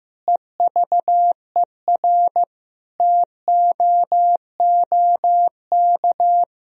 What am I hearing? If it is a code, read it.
Morse code, heard as a keyed tone, about 16 wpm, 700 Hz. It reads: EVER TOOK
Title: Sentences from Top 1000 Words - Review Part 1 - 15wpm